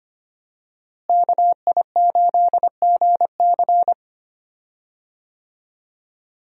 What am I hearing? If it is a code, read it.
Morse code, heard as a keyed tone, about 25 wpm, 700 Hz. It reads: KI8GC
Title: USA Call Signs - 5 Characters - Repeated 3x - 25wpm